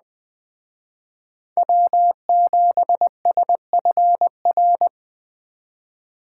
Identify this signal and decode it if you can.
Morse code, heard as a keyed tone, about 21 wpm, 700 Hz. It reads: W7SFR